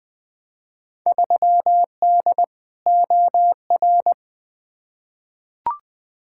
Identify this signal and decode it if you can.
Morse code, heard as a keyed tone, about 20 wpm, 700 Hz. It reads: 3D OR
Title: Field Day Exchanges Encore - 20wpm